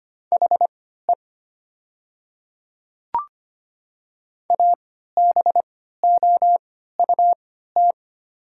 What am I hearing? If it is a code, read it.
Morse code, heard as a keyed tone, about 25 wpm, spaced as for 14 wpm, 700 Hz. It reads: HE ABOUT
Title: 3x Character Spacing - Top 100 Words - 25wpm